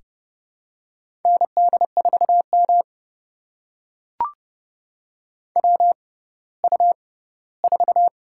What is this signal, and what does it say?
Morse code, heard as a keyed tone, about 30 wpm, spaced as for 17 wpm, 700 Hz. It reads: ND4M WU4
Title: USA Call Signs - 4 Characters - 6x Character Spacing - 30wpm